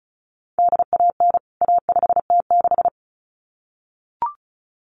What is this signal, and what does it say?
Morse code, heard as a keyed tone, about 34 wpm, 700 Hz. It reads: DAN A5T6
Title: CWT Exchange 35wpm